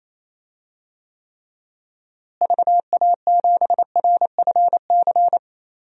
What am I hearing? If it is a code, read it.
Morse code, heard as a keyed tone, about 28 wpm, 700 Hz. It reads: VA7RFC
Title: DXCC-100 Call Signs - 6 Characters - Repeated 3x - 28wpm